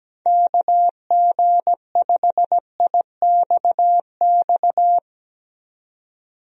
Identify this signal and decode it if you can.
Morse code, heard as a keyed tone, about 17 wpm, 700 Hz. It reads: KG5IXX
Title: USA Call Signs - 6 Characters - 17wpm